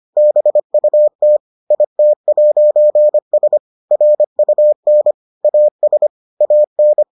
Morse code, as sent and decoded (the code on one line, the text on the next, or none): -... ..- - / .. - .----. ... / .-. ..- -. / .- ... / .- -.
BUT IT'S RUN AS AN